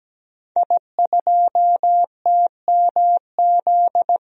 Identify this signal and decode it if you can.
Morse code, heard as a keyed tone, about 17 wpm, 700 Hz. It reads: I2TMZ
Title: DXCC-100 Call Signs - 5 Characters - Repeated 3x - 17wpm